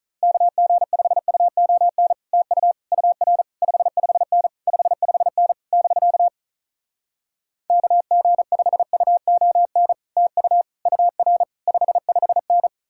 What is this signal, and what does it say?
Morse code, heard as a keyed tone, about 46 wpm, 700 Hz. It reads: KG5UON TU UR 55N 55N <BK> KG5UON TU UR 55N